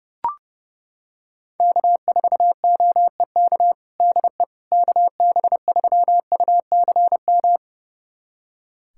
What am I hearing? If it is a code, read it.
Morse code, heard as a keyed tone, about 30 wpm, 700 Hz. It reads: K4OEK DE KB3UCM
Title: USA Callsign DE Callsign 30wpm